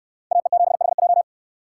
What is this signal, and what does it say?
Morse code, heard as a keyed tone, about 69 wpm, 700 Hz. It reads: AE7HJ